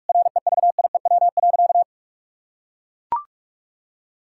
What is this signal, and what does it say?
Morse code, heard as a keyed tone, about 41 wpm, 700 Hz. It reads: REVIEW.
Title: Sentences from Top 1000 Words - Review Part 6 - 45wpm